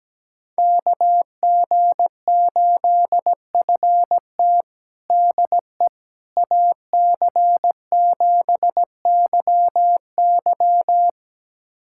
Morse code, heard as a keyed tone, about 17 wpm, 700 Hz. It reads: KG8FT DE AC7YY
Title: USA Callsign DE Callsign - 5 Characters - 17wpm